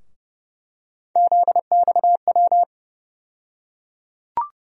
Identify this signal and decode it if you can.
Morse code, heard as a keyed tone, about 30 wpm, 700 Hz. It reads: ZXW
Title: USA Call Sign Postfixes - 3 Letters 30wpm